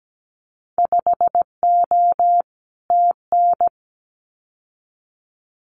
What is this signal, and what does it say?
Morse code, heard as a keyed tone, about 17 wpm, 700 Hz. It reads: 5O TN
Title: Winter Field Day Exchanges Encore 17wpm